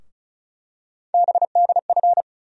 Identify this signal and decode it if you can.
Morse code, heard as a keyed tone, about 36 wpm, 700 Hz. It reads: BDF
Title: USA Call Sign Postfixes - 3 Letters 35wpm